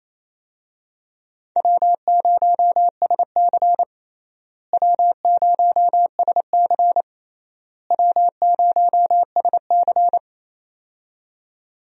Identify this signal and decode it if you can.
Morse code, heard as a keyed tone, about 27 wpm, 700 Hz. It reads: W0SC W0SC W0SC
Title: USA Call Signs - 4 Characters - Repeated 3x - 28wpm